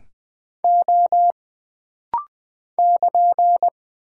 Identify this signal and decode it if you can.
Morse code, heard as a keyed tone, about 20 wpm, 700 Hz. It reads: O <KN>
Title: Single Character (Letter:Number:Punctuation:Prosign) - Rapid-Fire - 20wpm